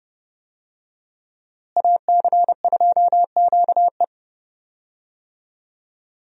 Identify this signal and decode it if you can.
Morse code, heard as a keyed tone, about 31 wpm, 700 Hz. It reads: AC2QE
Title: USA Call Signs - 5 Characters - Repeated 2x - 30wpm